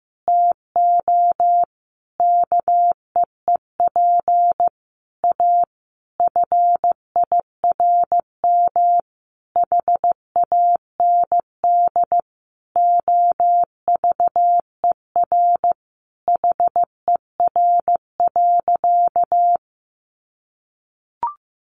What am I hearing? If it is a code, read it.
Morse code, heard as a keyed tone, about 15 wpm, 700 Hz. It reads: TO KEEP A FIRM HAND OVER HER.
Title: Sentences from Top 1400 Words 15wpm